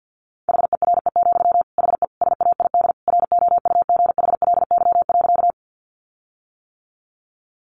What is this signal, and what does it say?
Morse code, heard as a keyed tone, about 50 wpm, 700 Hz. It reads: HERE, HE SAID ROUGHLY.
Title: Sentences from Top 1500 Words 50wpm